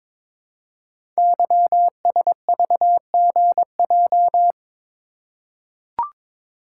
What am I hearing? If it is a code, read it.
Morse code, heard as a keyed tone, about 22 wpm, 700 Hz. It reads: YSVGJ
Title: Code Groups - 5 Letters - 22wpm